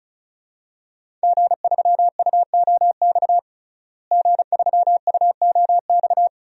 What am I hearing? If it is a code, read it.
Morse code, heard as a keyed tone, about 34 wpm, 700 Hz. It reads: G3UOX G3UOX
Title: DXCC-100 Call Signs - 5 Characters - Repeated 2x - 35wpm